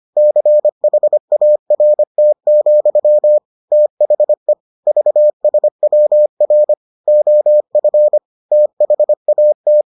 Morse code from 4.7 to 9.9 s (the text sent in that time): VSWR OF THAT